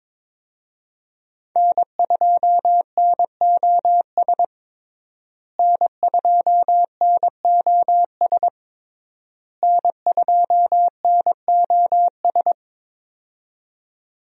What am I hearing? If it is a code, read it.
Morse code, heard as a keyed tone, about 22 wpm, 700 Hz. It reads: N2NOS N2NOS N2NOS
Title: USA Call Signs - 5 Characters - Repeated 3x - 22wpm